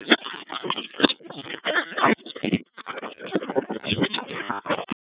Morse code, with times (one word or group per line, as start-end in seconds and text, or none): none